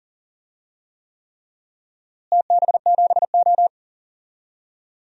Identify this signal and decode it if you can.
Morse code, heard as a keyed tone, about 40 wpm, 700 Hz. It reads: TB7O
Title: DXCC-100 Call Signs - 4 Characters - Repeated 3x - 40wpm